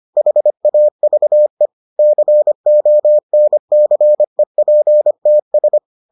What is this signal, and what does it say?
Morse code, heard as a keyed tone, about 25 wpm, 600 Hz. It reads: HAVE CONCEPTS